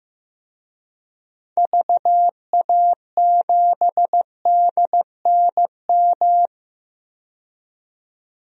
Morse code, heard as a keyed tone, about 15 wpm, 700 Hz. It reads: VA7DNM
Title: DXCC-100 Call Signs - 6 Characters - 15wpm